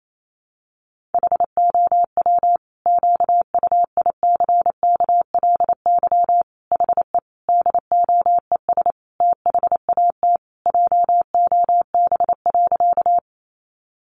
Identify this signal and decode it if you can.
Morse code, heard as a keyed tone, about 28 wpm, 700 Hz. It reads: HOW QUICKLY HE DOES THAT JOB.
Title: Sentences from Top 700 Words 28wpm